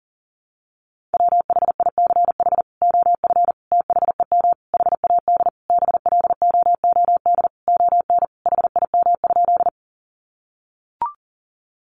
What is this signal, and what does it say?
Morse code, heard as a keyed tone, about 40 wpm, 700 Hz. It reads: WHICH OF THEM HAD BLOOD ON HIM?
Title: Sentences from Top 1100 Words 40wpm